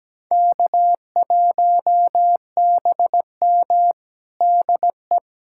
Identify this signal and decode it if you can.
Morse code, heard as a keyed tone, about 17 wpm, 700 Hz. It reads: K1BM DE
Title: USA Callsign DE Callsign - 4 Characters - 17wpm